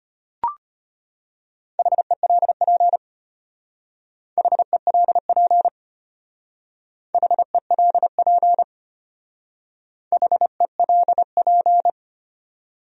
Morse code, heard as a keyed tone, about 31 wpm, 700 Hz. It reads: HELP HELP HELP HELP
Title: Speed-Racing Top 300 Words 25wpm